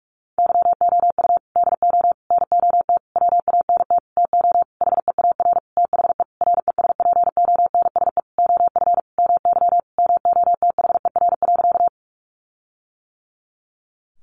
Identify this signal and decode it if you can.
Morse code, heard as a keyed tone, about 44 wpm, 700 Hz. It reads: YOU DO NOT WANT TO HEAR THE RESPONSE OF MY MOTHER.
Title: Sentences from Top 1400 Words 45wpm